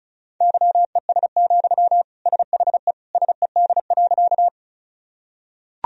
Morse code, heard as a keyed tone, about 34 wpm, 700 Hz. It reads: YES, SHE SED.
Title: Sentences from Top 500 Words and Abbreviations 35wpm